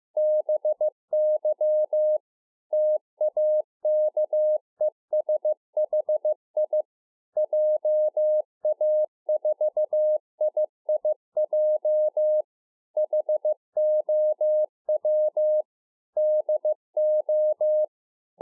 Morse code, heard as a keyed tone, about 15 wpm, 610 Hz. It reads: BY TAKESHI JA4IIJ HOW DO